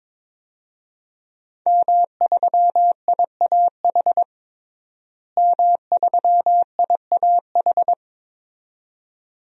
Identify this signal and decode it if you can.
Morse code, heard as a keyed tone, about 22 wpm, 700 Hz. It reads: M3IAH M3IAH
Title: DXCC-100 Call Signs - 5 Characters - Repeated 2x - 22wpm